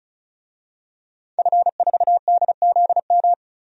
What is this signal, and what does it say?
Morse code, heard as a keyed tone, about 34 wpm, 700 Hz. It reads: F4DZM
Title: DXCC-100 Call Signs - 5 Characters - 35wpm